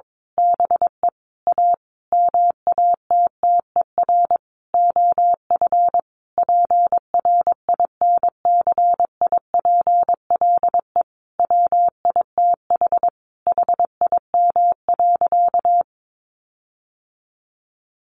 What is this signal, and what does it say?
Morse code, heard as a keyed tone, about 22 wpm, 700 Hz. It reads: BE A MATTER OF PRINCIPLE WITH HIM.
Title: Sentences from Top 1400 Words 22wpm